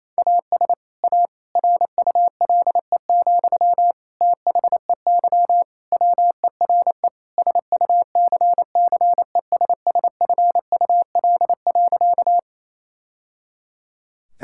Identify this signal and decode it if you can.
Morse code, heard as a keyed tone, about 27 wpm, 700 Hz. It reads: AS A RULE, THEY WERE SUCCESSFUL.
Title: Sentences from Top 1000 Words - Review Part 6 - 28wpm